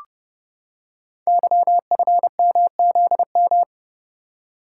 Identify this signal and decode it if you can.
Morse code, heard as a keyed tone, about 29 wpm, 700 Hz. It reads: YFMZM
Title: Code Groups - 5 Letters - 30wpm